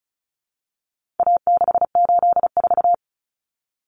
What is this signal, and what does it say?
Morse code, heard as a keyed tone, about 34 wpm, 700 Hz. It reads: A684
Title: Common Cut Numbers - 4-Digit 35wpm